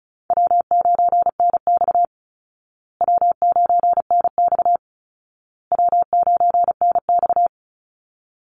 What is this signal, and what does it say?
Morse code, heard as a keyed tone, about 34 wpm, 700 Hz. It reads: W9NX W9NX W9NX
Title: Easy Call Signs Repeated 3x - 35wpm